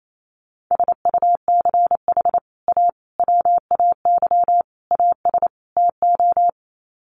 Morse code, heard as a keyed tone, about 28 wpm, 700 Hz. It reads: SUCH A WAY AS TO
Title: Sets of 5 Words Encore2 28wpm